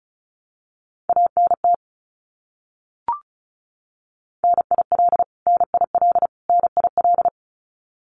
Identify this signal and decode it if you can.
Morse code, heard as a keyed tone, about 34 wpm, 700 Hz. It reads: ANT NIL NIL NIL
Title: Abbreviations Repeated 3x - 35wpm